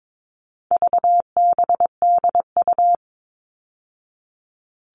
Morse code, heard as a keyed tone, about 22 wpm, 700 Hz. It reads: VBDU
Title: All Cut Numbers - 4-Digit 22wpm